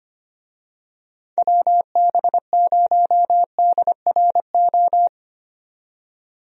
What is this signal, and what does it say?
Morse code, heard as a keyed tone, about 25 wpm, 700 Hz. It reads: WB0DRO
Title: USA Call Signs - 6 Characters - 25wpm